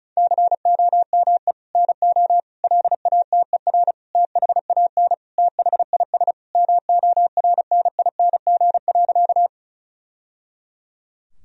Morse code, heard as a keyed tone, about 34 wpm, 700 Hz. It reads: COME NO LATER THAN THIS MORNING.
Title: Sentences from Top 1000 Words - Review Part 4 - 35wpm